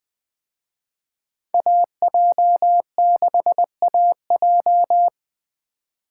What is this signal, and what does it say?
Morse code, heard as a keyed tone, about 24 wpm, 700 Hz. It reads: AJ6AJ
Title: Speed Racing - USA Call Signs 20wpm